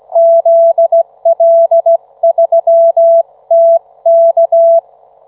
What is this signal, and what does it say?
Morse code, heard as a keyed tone, about 23 wpm, spaced as for 18 wpm, 660 Hz. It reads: ZL3TK